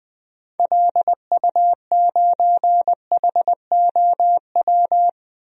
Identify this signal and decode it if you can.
Morse code, heard as a keyed tone, about 20 wpm, 700 Hz. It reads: LU9HOW
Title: DXCC-100 Call Signs - 6 Characters - 20wpm